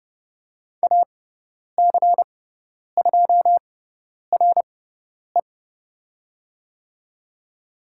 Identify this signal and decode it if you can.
Morse code, heard as a keyed tone, about 30 wpm, spaced as for 10 wpm, 690 Hz. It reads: AC2RE
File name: Lesson 18g - Letter m - Call Signs 30wpm